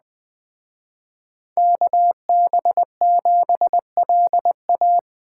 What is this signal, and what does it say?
Morse code, heard as a keyed tone, about 20 wpm, 700 Hz. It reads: KB7LA
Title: Speed Racing - USA 2x2 Call Signs 17wpm